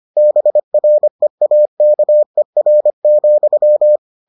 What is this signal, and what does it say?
Morse code, heard as a keyed tone, about 25 wpm, 600 Hz. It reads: BREAKER,